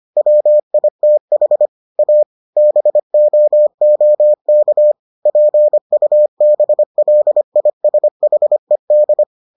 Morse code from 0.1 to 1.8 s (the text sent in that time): WITH